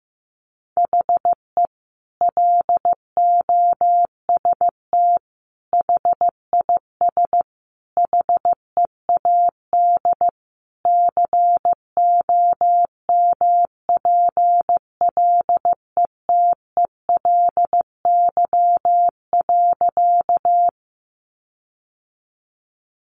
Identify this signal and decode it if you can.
Morse code, heard as a keyed tone, about 15 wpm, 700 Hz. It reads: HE LOST HIS HEAD COMPLETELY.
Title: Sentences from Top 1400 Words 15wpm